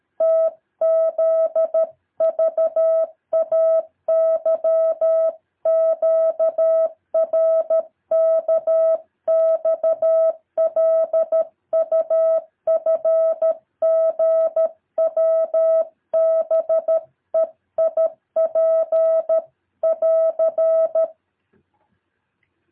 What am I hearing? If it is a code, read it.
Morse code, heard as a keyed tone, about 13 wpm, 650 Hz. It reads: TZVAYQRKXLUFGWBEIP <AR>
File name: jh8cbh / 2023cw